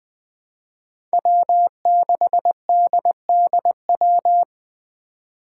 Morse code, heard as a keyed tone, about 20 wpm, 700 Hz. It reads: W6DDW
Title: Five Challenging Letter-Number Combinations 20wpm